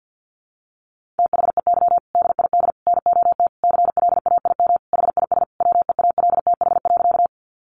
Morse code, heard as a keyed tone, about 50 wpm, 700 Hz. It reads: THEY DID NOT CLAIM HIS WEALTH.
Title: Sentences from Top 1500 Words 50wpm